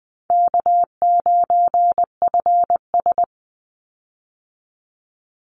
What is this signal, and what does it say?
Morse code, heard as a keyed tone, about 20 wpm, 700 Hz. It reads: K9FS
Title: Easy Call Signs Repeated 2x - 20wpm